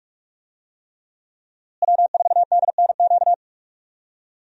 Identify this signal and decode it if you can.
Morse code, heard as a keyed tone, about 42 wpm, 700 Hz. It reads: W4DNQ
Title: USA Call Signs - 5 Characters - Repeated 3x - 45wpm